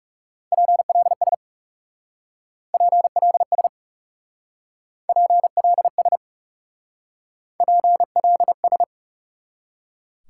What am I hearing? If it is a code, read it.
Morse code, heard as a keyed tone, about 35 wpm, 700 Hz. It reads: PLS PLS PLS PLS
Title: Speed-Racing Top 180 QSO Elements 30wpm